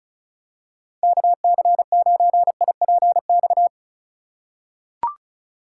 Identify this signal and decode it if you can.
Morse code, heard as a keyed tone, about 36 wpm, 700 Hz. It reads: KC9IPX
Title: USA Call Signs - 6 Characters - 2x Character Spacing - 35wpm